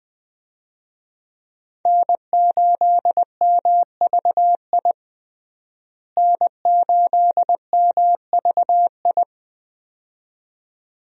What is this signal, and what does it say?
Morse code, heard as a keyed tone, about 20 wpm, 700 Hz. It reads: N8MVI N8MVI
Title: USA Call Signs - 5 Characters - Repeated 2x - 20wpm